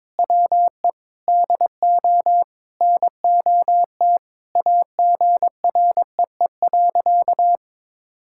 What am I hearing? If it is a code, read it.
Morse code, heard as a keyed tone, about 22 wpm, 700 Hz. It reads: WE DO NOT AGREE.
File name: practice / 22wpm / Sentences with 4-Word Phrases - 22wpm